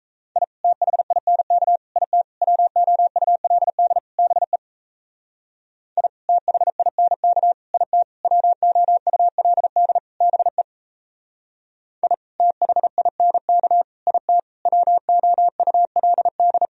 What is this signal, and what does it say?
Morse code, heard as a keyed tone, about 36 wpm, 700 Hz. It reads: I THINK IT WOULD BE I THINK IT WOULD BE I THINK IT WOULD